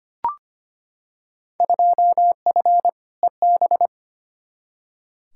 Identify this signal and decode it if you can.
Morse code, heard as a keyed tone, about 25 wpm, 700 Hz. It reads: 2F EB
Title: Field Day Exchanges Encore - 25wpm